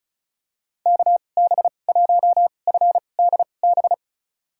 Morse code, heard as a keyed tone, about 36 wpm, spaced as for 25 wpm, 700 Hz. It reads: KB1FDB
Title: USA Call Signs - 2x Character Spacing - 35wpm